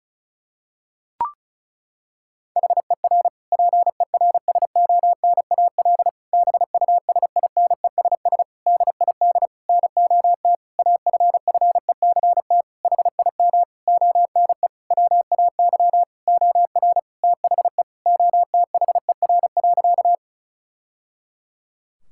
Morse code, heard as a keyed tone, about 36 wpm, 700 Hz. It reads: HER PERSONAL BUSINESS DID NOT AFFECT HIM ONE WAY OR THE OTHER.
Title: Sentences from Top 1000 Words - Review Part 6 - 35wpm